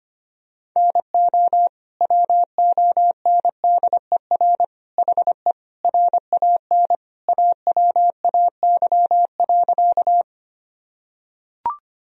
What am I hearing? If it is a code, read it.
Morse code, heard as a keyed tone, about 25 wpm, 700 Hz. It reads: NO WONDER HE RAN AWAY.
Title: Sentences from Top 1000 Words - Review Part 3 - 25wpm